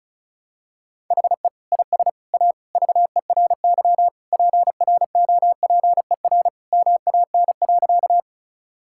Morse code, heard as a keyed tone, about 34 wpm, 700 Hz. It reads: HE IS A VERY PROPER MAN.
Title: Sentences from Top 1000 Words - Review Part 5 - 35wpm